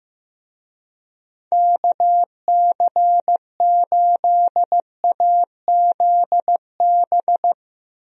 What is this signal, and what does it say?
Morse code, heard as a keyed tone, about 15 wpm, 700 Hz. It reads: KC8AZB
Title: USA Call Signs - 6 Characters - 15wpm